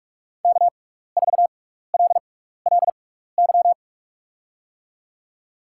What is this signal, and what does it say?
Morse code, heard as a keyed tone, about 44 wpm, spaced as for 15 wpm, 700 Hz. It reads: K4LLY
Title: USA Call Signs - 6x Character Spacing - 45wpm